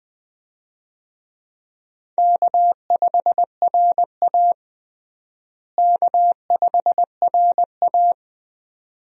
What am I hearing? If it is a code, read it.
Morse code, heard as a keyed tone, about 20 wpm, 700 Hz. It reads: K5RA K5RA
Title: USA Call Signs - 4 Characters - Repeated 2x - 20wpm